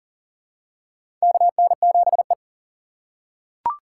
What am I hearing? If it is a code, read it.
Morse code, heard as a keyed tone, about 40 wpm, 700 Hz. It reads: KN7E
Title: USA Call Signs - 3x Character Spacing - 40wpm